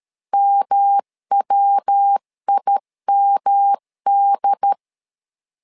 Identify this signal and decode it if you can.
Morse code, heard as a keyed tone, about 13 wpm, 800 Hz. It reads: MWIMD
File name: LEZIONI / 3 MWRO / 3 MWR 12wpm 1711